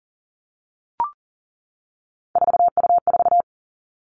Morse code, heard as a keyed tone, about 38 wpm, 700 Hz. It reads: 4U4
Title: Three Challenging Letter-Number Combinations 40wpm